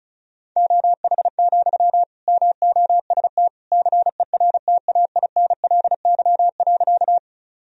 Morse code, heard as a keyed tone, about 36 wpm, 700 Hz. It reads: OH, MOST CERTAINLY.
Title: Sentences from Top 1000 Words - Review Part 1 - 35wpm